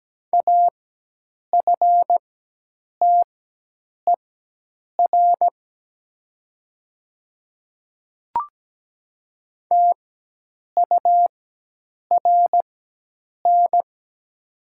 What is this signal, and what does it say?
Morse code, heard as a keyed tone, about 17 wpm, spaced as for 8 wpm, 690 Hz. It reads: AFTER TURN
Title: 4x Character Spacing - Top 300 Words - 17wpm